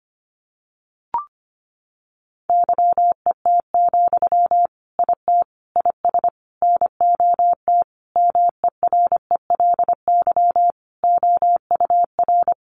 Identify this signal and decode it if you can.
Morse code, heard as a keyed tone, about 25 wpm, 700 Hz. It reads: YET, IT IS NOT MERELY OUR
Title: Sentences from Top 1400 Words 25wpm